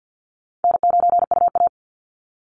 Morse code, heard as a keyed tone, about 50 wpm, 700 Hz. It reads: N9UA
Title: Easy Call Signs Repeated 2x - 50wpm